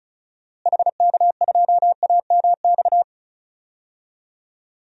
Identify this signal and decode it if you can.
Morse code, heard as a keyed tone, about 36 wpm, 700 Hz. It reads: HK2AMX
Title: DXCC-100 Call Signs - 6 Characters - Repeated 2x - 35wpm